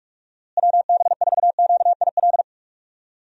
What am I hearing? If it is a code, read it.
Morse code, heard as a keyed tone, about 47 wpm, 700 Hz. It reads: WB4QIL
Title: USA Call Signs - 6 Characters - Repeated 3x - 45wpm